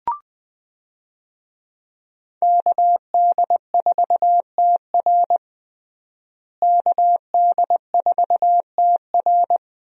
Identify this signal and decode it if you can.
Morse code, heard as a keyed tone, about 21 wpm, 700 Hz. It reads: KD4TR KD4TR
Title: USA Call Signs - 5 Characters - Repeated 3x - 20wpm